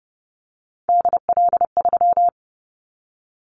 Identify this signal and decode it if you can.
Morse code, heard as a keyed tone, about 30 wpm, 700 Hz. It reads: DL3
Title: Callsign Prefixes for Top 25 DXCC - 30wpm